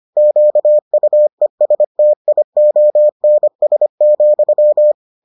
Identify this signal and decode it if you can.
Morse code, heard as a keyed tone, about 25 wpm, 600 Hz. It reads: QUESTIONS,